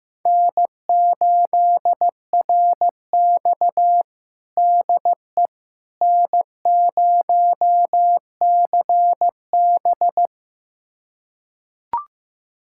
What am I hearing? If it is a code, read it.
Morse code, heard as a keyed tone, about 15 wpm, 700 Hz. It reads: N8RX DE N0CB
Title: USA Callsign DE Callsign - 4 Characters - 15wpm